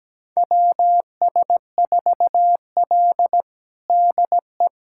Morse code, heard as a keyed tone, about 17 wpm, 700 Hz. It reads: WS4L DE
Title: USA Callsign DE Callsign - 4 Characters - 17wpm